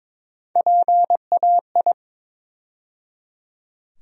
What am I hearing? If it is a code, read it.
Morse code, heard as a keyed tone, about 22 wpm, 700 Hz. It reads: PAI
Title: USA Call Sign Postfixes - 3 Letters 22wpm